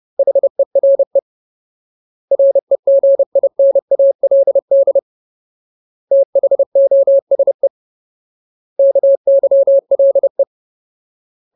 Morse code, heard as a keyed tone, about 30 wpm, 550 Hz. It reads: HERE REGINALD THOSE KYLE
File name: Ultimate Word List / 30WPM Ultimate